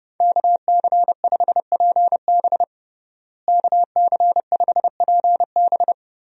Morse code, heard as a keyed tone, about 32 wpm, 700 Hz. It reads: KC5PB KC5PB
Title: USA Call Signs - 5 Characters - Repeated 3x - 30wpm